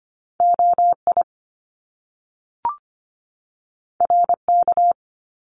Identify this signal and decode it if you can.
Morse code, heard as a keyed tone, about 25 wpm, 700 Hz. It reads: OI RK
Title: Top Two Letter Combinations in Words 25wpm